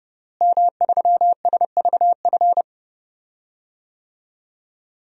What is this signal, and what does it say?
Morse code, heard as a keyed tone, about 30 wpm, 700 Hz. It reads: M3SVF
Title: DXCC-100 Call Signs - 5 Characters - Repeated 2x - 30wpm